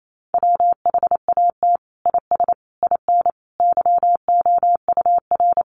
Morse code, heard as a keyed tone, about 28 wpm, 700 Hz. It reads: WHAT IS IN YOUR
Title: Sentences from Top 200 Words 28wpm